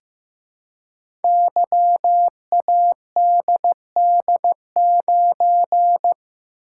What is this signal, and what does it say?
Morse code, heard as a keyed tone, about 15 wpm, 700 Hz. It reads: YADD9
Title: Code Groups - 5 Letter:Numbers - 15wpm